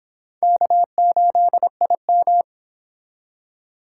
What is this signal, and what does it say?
Morse code, heard as a keyed tone, about 24 wpm, 700 Hz. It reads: K8IM